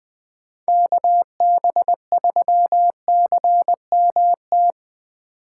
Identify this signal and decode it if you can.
Morse code, heard as a keyed tone, about 20 wpm, 700 Hz. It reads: KB3CMT
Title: 2023 POTA Activator Call Signs - 20wpm